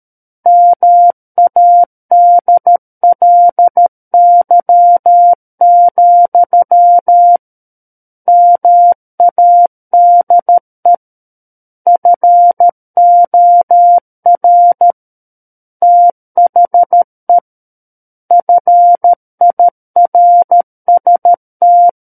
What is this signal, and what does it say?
Morse code, heard as a keyed tone, about 13 wpm, 700 Hz. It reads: MADLY, MADE FOR THE FIRST
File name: Book One / 04-Chapter 4 - 13 WPM